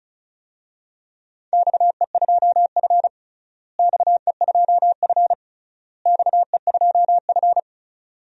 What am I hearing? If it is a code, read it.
Morse code, heard as a keyed tone, about 36 wpm, 690 Hz. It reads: XE2F XE2F XE2F
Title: DXCC-100 Call Signs - 4 Characters - Repeated 3x - 35wpm